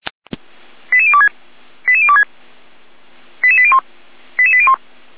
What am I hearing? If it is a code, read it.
Five-tone selective call (ZVEI), heard as quick runs of five tones, tones of about 70 ms. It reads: 88026 88026 88801 88801